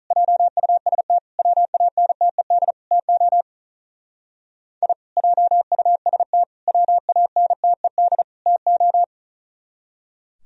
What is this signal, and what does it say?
Morse code, heard as a keyed tone, about 46 wpm, 700 Hz. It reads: JUST WANTED TO I JUST WANTED TO